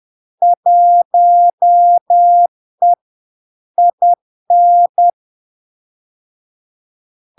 Morse code, heard as a keyed tone, about 10 wpm, 700 Hz. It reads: 1E IN
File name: LARK-FD-2022-Log-in-CW-10wpm-700-Hz